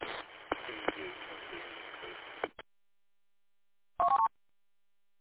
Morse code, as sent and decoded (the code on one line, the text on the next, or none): none